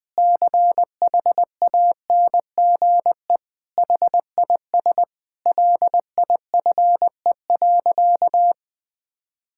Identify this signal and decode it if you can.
Morse code, heard as a keyed tone, about 20 wpm, 700 Hz. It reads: CHANGE HIS LIFE.
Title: Sentences from Top 1000 Words - Review Part 5 - 20wpm